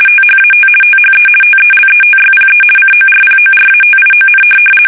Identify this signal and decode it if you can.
RTTY, about 50 baud. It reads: THAT THE TELETYPE CORPORATION
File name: fsk850-50B / teletype-corp